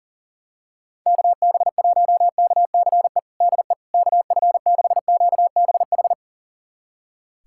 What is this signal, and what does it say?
Morse code, heard as a keyed tone, about 40 wpm, 700 Hz. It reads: KB1KCE DE KF6QBH
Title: USA Callsign DE Callsign 40wpm